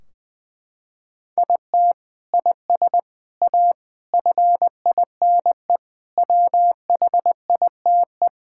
Morse code, heard as a keyed tone, about 20 wpm, 700 Hz. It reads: IT IS A FINE WHITE
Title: Sentences from Top 1600 Words 20wpm